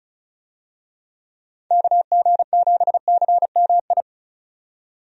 Morse code, heard as a keyed tone, about 34 wpm, 700 Hz. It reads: KG7CMI
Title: USA Call Signs - 6 Characters - 35wpm